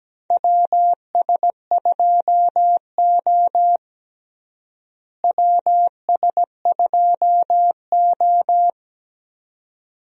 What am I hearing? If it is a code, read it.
Morse code, heard as a keyed tone, about 17 wpm, 700 Hz. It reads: WS2O WS2O
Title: USA Call Signs - 4 Characters - Repeated 3x - 17wpm